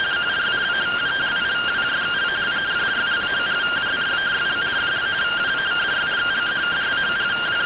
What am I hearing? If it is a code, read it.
Two-tone teleprinter signal, about 170 Hz shift.